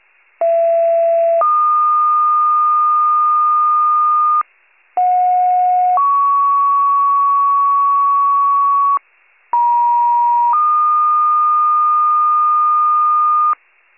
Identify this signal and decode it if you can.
Two-tone page, heard as a one-second tone, then a three-second tone, about 670 then 1220 Hz, about 730 then 1120 Hz, about 950 then 1220 Hz. It is heard from 0.4 to 13.5 s.
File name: pager / 39 RESCUE / wcalert